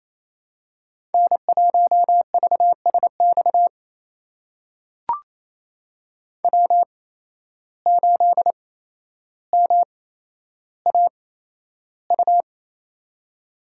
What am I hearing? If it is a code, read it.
Morse code, heard as a keyed tone, about 28 wpm, spaced as for 12 wpm, 700 Hz. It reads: N1VSX W8MAU
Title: USA Call Signs - 5 Characters - 8x Character Spacing - 28wpm